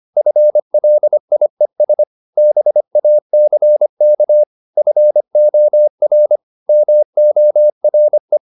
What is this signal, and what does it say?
Morse code, heard as a keyed tone, about 25 wpm, 600 Hz. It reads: FLIES BACK FOR MORE